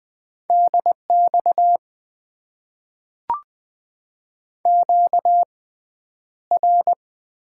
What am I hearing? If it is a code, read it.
Morse code, heard as a keyed tone, about 20 wpm, spaced as for 10 wpm, 700 Hz. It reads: DX QR
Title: Top 90 QSO Elements - 6x Character Spacing - 20wpm